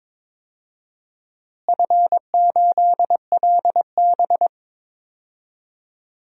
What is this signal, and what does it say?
Morse code, heard as a keyed tone, about 22 wpm, 700 Hz. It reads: F8LB
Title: DXCC-100 Call Signs - 4 Characters - Repeated 2x - 22wpm